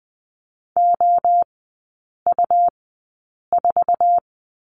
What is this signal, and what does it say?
Morse code, heard as a keyed tone, about 20 wpm, spaced as for 8 wpm, 700 Hz. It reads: OU4
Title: 121 Lesson 20e - Number 3 - Three Characters 20wpm